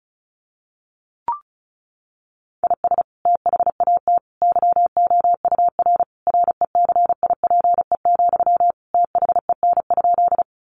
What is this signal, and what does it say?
Morse code, heard as a keyed tone, about 34 wpm, 700 Hz. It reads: IS THAT YOUR RECIPE, THEN?
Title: Sentences from Top 1200 Words 35wpm